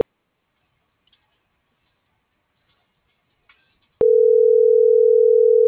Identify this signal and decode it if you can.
Ringback tone (North America), 440 + 480 Hz.